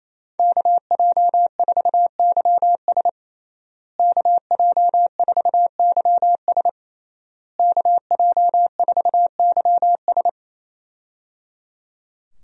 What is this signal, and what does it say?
Morse code, heard as a keyed tone, about 28 wpm, 700 Hz. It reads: KJ4YS KJ4YS KJ4YS
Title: USA Call Signs - 5 Characters - Repeated 3x - 28wpm